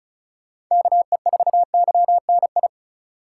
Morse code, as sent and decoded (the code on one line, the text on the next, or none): -.- . ....- -.-- -. ..
KE4YNI